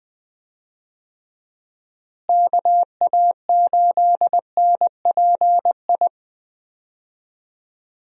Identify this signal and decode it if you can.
Morse code, heard as a keyed tone, about 20 wpm, 700 Hz. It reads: KA8NPI
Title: USA Call Signs - 6 Characters - Repeated 2x - 20wpm